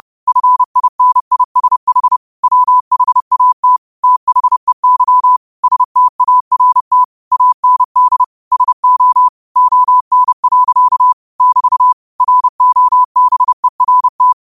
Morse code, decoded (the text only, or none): FINISH WHAT THEY START AND SO ON. <BT> ROBERT